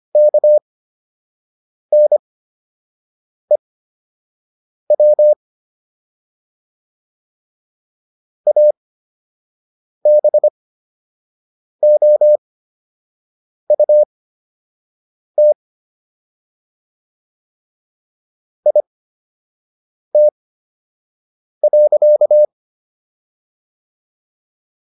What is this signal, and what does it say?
Morse code, heard as a keyed tone, about 25 wpm, spaced as for 6 wpm, 600 Hz. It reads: KNEW ABOUT IT.